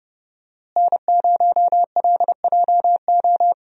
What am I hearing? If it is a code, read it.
Morse code, heard as a keyed tone, about 30 wpm, 700 Hz. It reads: N0LJO
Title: USA Call Signs - 5 Characters - Repeated 3x - 30wpm